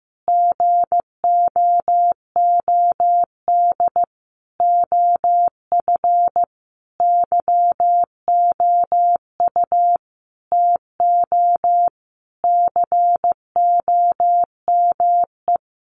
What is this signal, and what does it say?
Morse code, heard as a keyed tone, about 15 wpm, 700 Hz. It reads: GOOD OF YOU TO COME
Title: Sentences from Top 1100 Words 15wpm